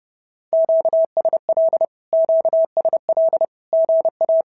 Morse code, heard as a keyed tone, about 30 wpm, 650 Hz. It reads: QSL QSL GA
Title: SOTA QSO Practice 30wpm